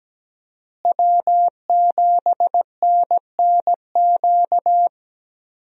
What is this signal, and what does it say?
Morse code, heard as a keyed tone, about 17 wpm, 700 Hz. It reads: W7NNQ
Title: USA Call Signs - 5 Characters - Repeated 2x - 17wpm